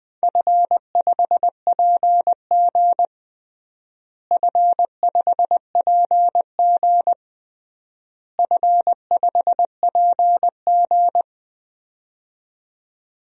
Morse code, heard as a keyed tone, about 20 wpm, 700 Hz. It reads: F5PG F5PG F5PG
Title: DXCC-100 Call Signs - 4 Characters - Repeated 3x - 20wpm